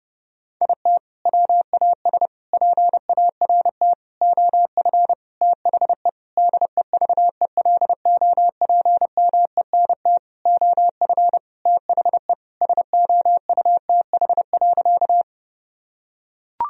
Morse code, heard as a keyed tone, about 32 wpm, 700 Hz. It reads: IT WAS PART OF THE DEVELOPMENT OF THE SOUTH.
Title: Sentences from Top 1000 Words - Review Part 4 - 30wpm